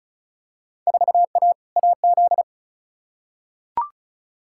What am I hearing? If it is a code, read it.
Morse code, heard as a keyed tone, about 36 wpm, 700 Hz. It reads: 4A AZ
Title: Field Day Exchanges Encore - 35wpm